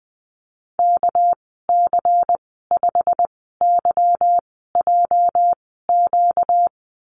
Morse code, heard as a keyed tone, about 20 wpm, spaced as for 14 wpm, 700 Hz. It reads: KC5YJQ
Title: 245 Lesson 39g - Letter z - Call Signs 20wpm